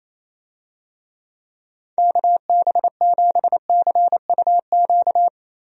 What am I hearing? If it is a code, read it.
Morse code, heard as a keyed tone, about 28 wpm, 700 Hz. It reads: KB7CUQ
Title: USA Call Signs - 6 Characters - Repeated 3x - 28wpm